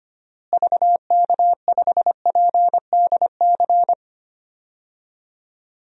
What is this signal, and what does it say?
Morse code, heard as a keyed tone, about 25 wpm, 700 Hz. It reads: VK5PDC
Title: DXCC-100 Call Signs - 6 Characters - Repeated 2x - 25wpm